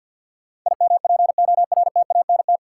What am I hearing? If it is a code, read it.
Morse code, heard as a keyed tone, about 49 wpm, 700 Hz. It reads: IMPORTANT
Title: Sets of 3 Words - New Word Drill - 50wpm